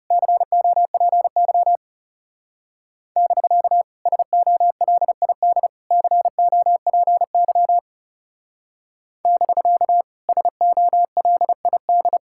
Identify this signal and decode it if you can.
Morse code, heard as a keyed tone, about 38 wpm, 700 Hz. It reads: COPY <BK> SOLID COPY <BK> SOLID